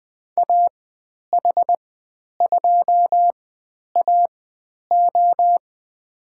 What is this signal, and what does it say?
Morse code, heard as a keyed tone, about 20 wpm, spaced as for 10 wpm, 700 Hz. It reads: AH2AO
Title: 095 Lesson 16g - Letter c - Call Signs 20wpm